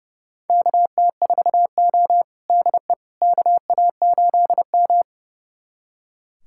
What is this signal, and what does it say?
Morse code, heard as a keyed tone, about 30 wpm, 700 Hz. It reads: KT4O DE KA8M
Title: USA Callsign DE Callsign - 4 Characters - 30wpm